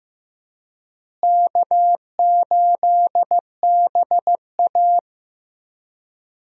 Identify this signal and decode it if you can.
Morse code, heard as a keyed tone, about 15 wpm, 700 Hz. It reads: K8BA
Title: USA Call Signs - 4 Characters - Repeated 3x - 15wpm